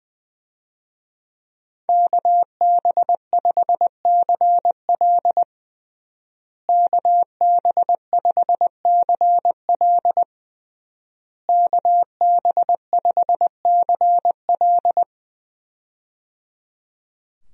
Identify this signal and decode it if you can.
Morse code, heard as a keyed tone, about 20 wpm, 700 Hz. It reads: KB5CL KB5CL KB5CL
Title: USA Call Signs - 5 Characters - Repeated 3x - 20wpm